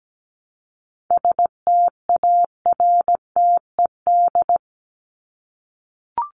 Morse code, heard as a keyed tone, about 17 wpm, 700 Hz. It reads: STARTED
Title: ED Suffix - Top 100 Words with Base Words 17wpm